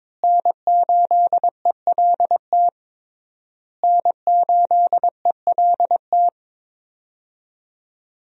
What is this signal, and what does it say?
Morse code, heard as a keyed tone, about 22 wpm, 690 Hz. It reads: N8ELT N8ELT
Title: USA Call Signs - 5 Characters - Repeated 3x - 22wpm